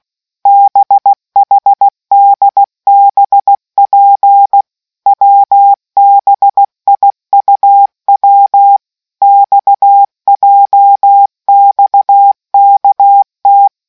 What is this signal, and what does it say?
Morse code, heard as a keyed tone, about 16 wpm, 800 Hz. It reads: BHDBP WBIUW XJXKT